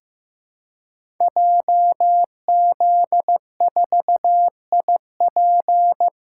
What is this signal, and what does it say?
Morse code, heard as a keyed tone, about 15 wpm, 700 Hz. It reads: JZ4IP
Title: Code Groups - 5 Letter:Numbers - 15wpm